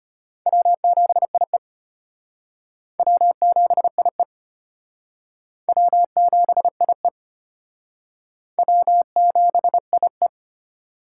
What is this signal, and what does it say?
Morse code, heard as a keyed tone, about 33 wpm, 700 Hz. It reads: W7IE W7IE W7IE W7IE